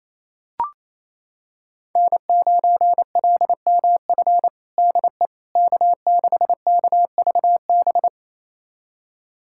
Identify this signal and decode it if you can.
Morse code, heard as a keyed tone, about 28 wpm, 700 Hz. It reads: N9LMF DE K6KVB
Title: USA Callsign DE Callsign - 5 Characters - 28wpm